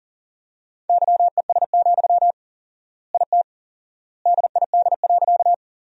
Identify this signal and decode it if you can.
Morse code, heard as a keyed tone, about 37 wpm, 700 Hz. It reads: YES, IT DID.
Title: 4x Word-Spacing - Sentences from Top 500 Words - 40wpm